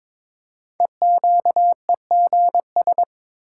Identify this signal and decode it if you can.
Morse code, heard as a keyed tone, about 22 wpm, 700 Hz. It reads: EQEGS
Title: Code Groups - 5 Letters - 22wpm